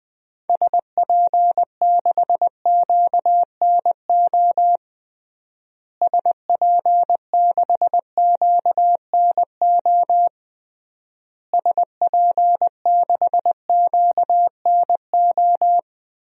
Morse code, heard as a keyed tone, about 21 wpm, 700 Hz. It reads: SP6QNO SP6QNO SP6QNO
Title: DXCC-100 Call Signs - 6 Characters - Repeated 3x - 20wpm